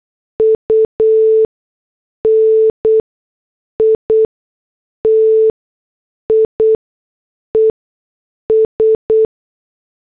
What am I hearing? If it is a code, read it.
Morse code, heard as a keyed tone, about 8 wpm, spaced as for 6 wpm, 440 Hz. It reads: UNITIES